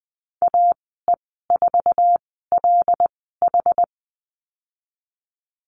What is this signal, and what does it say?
Morse code, heard as a keyed tone, about 19 wpm, spaced as for 14 wpm, 700 Hz. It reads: AE4LH
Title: 173 Lesson 28g - Number 9 - Call Signs 20wpm